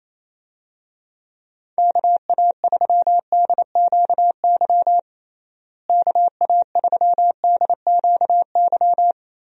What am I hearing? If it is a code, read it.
Morse code, heard as a keyed tone, about 28 wpm, 700 Hz. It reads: KA3DQY KA3DQY
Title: USA Call Signs - 6 Characters - Repeated 3x - 28wpm